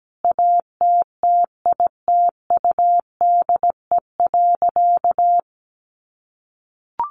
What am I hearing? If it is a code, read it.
Morse code, heard as a keyed tone, about 17 wpm, 700 Hz. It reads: ATTITUDE.
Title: Sentences from Top 1300 Words 17wpm